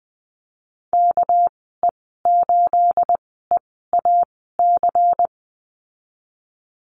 Morse code, heard as a keyed tone, about 20 wpm, spaced as for 15 wpm, 700 Hz. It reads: KE8EAC
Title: 231 Lesson 37g - Letter x - Call Signs 20wpm